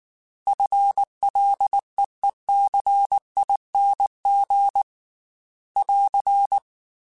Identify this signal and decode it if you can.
Morse code, heard as a keyed tone, about 19 wpm, 800 Hz. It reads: FLEECING <AR>
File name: FL / FL mp3s / FL Prefixes 3x2 @18